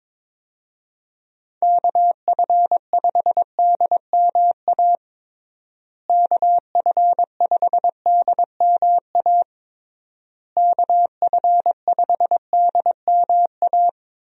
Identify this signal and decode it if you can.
Morse code, heard as a keyed tone, about 22 wpm, 700 Hz. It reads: KF5DMA KF5DMA KF5DMA
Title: USA Call Signs - 6 Characters - Repeated 3x - 22wpm